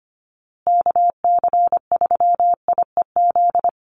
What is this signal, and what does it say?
Morse code, heard as a keyed tone, about 25 wpm, 700 Hz. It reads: KC3IEZ
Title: USA Call Signs Repeated 3x 25wpm